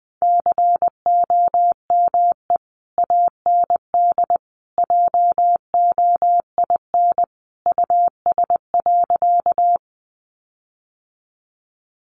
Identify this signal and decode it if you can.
Morse code, heard as a keyed tone, about 20 wpm, 700 Hz. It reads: COME AND JOIN US.
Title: Sentences from Top 1400 Words 20wpm